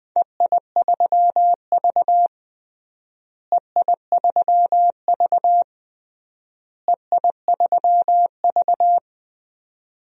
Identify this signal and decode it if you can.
Morse code, heard as a keyed tone, about 20 wpm, 700 Hz. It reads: EI3V EI3V EI3V
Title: DXCC-100 Call Signs - 4 Characters - Repeated 3x - 20wpm